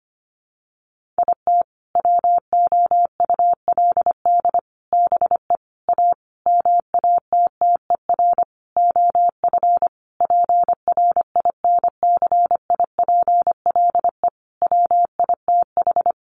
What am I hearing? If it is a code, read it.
Morse code, heard as a keyed tone, about 25 wpm, 700 Hz. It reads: IT WOULD BE A MATTER OF PRINCIPLE WITH
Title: Sentences from Top 1400 Words 25wpm